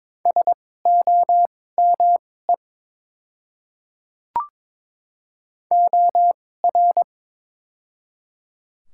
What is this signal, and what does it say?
Morse code, heard as a keyed tone, about 21 wpm, spaced as for 16 wpm, 700 Hz. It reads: SOME OR
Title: 2x Character Spacing - Top 100 Words - 22wpm